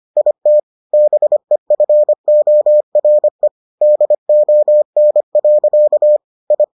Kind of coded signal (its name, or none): Morse code